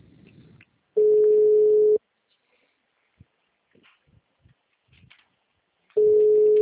Ringback tone (Europe, 425 Hz).
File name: out- -2006- - - .4567